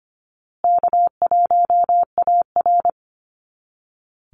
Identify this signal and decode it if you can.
Morse code, heard as a keyed tone, about 25 wpm, 700 Hz. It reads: K1AR
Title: Easy Call Signs Repeated 3x - 25wpm